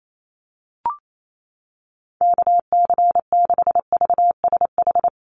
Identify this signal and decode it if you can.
Morse code, heard as a keyed tone, about 28 wpm, 700 Hz. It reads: KC6VSH
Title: USA Call Signs Repeated 2x 28wpm